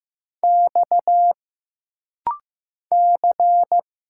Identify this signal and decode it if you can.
Morse code, heard as a keyed tone, about 15 wpm, 700 Hz. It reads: X C
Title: Single Letters - Rapid-Fire 15wpm